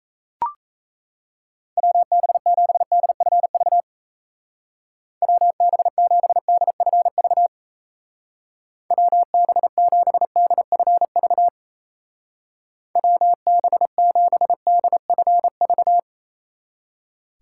Morse code, heard as a keyed tone, about 35 wpm, 700 Hz. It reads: WB7DFV WB7DFV WB7DFV WB7DFV
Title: Speed Racing - USA Call Signs 28wpm